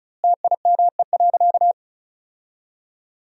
Morse code, heard as a keyed tone, about 34 wpm, 700 Hz. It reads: TIME.
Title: 2x Word-Spacing - Sentences from Top 200 Words - 35wpm